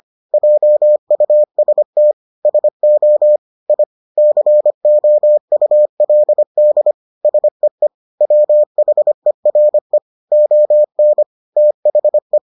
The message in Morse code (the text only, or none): JUST SO I COULD SEE WHERE ON THE